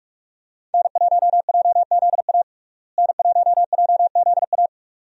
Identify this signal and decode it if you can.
Morse code, heard as a keyed tone, about 42 wpm, 700 Hz. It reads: N1JZA N1JZA
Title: USA Call Signs - 5 Characters - Repeated 2x - 45wpm